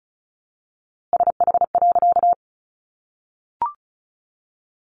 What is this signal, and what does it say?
Morse code, heard as a keyed tone, about 34 wpm, 700 Hz. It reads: SH.
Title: A Sound Of Thunder 35wpm